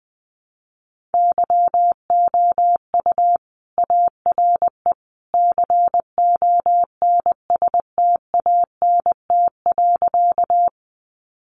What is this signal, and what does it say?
Morse code, heard as a keyed tone, about 20 wpm, 700 Hz. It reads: YOU ARE CONSTANT.
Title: Sentences from Top 1500 Words 20wpm